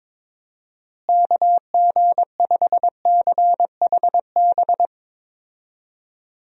Morse code, heard as a keyed tone, about 22 wpm, 700 Hz. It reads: KG5CHB
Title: USA Call Signs - 6 Characters - Repeated 2x - 22wpm